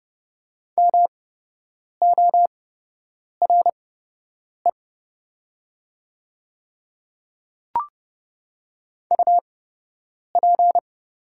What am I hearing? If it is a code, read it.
Morse code, heard as a keyed tone, about 30 wpm, spaced as for 8 wpm, 700 Hz. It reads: MORE UP
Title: 8x Character Spacing - Top 200 Words - 30wpm